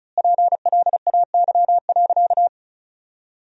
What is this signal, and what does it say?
Morse code, heard as a keyed tone, about 34 wpm, 700 Hz. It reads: PLAY.
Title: 2x Word-Spacing - Sentences from Top 200 Words - 35wpm